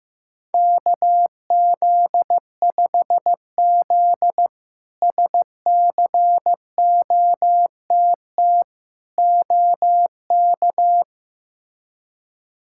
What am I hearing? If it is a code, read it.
Morse code, heard as a keyed tone, about 15 wpm, 700 Hz. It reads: KZ5Z SCOTT OK
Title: K1USN SST QSO Practice - 15wpm